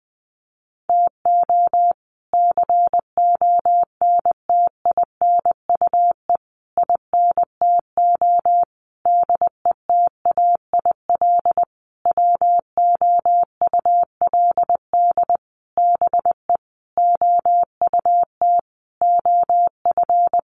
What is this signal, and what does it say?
Morse code, heard as a keyed tone, about 20 wpm, 700 Hz. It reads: TO CONTINUE INTO DETAIL WOULD BE OUT OF
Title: Sentences from Top 1500 Words 20wpm